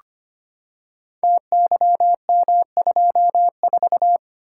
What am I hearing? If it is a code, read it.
Morse code, heard as a keyed tone, about 25 wpm, 700 Hz. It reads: TYM24
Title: Code Groups - 5 Letter:Numbers - 25wpm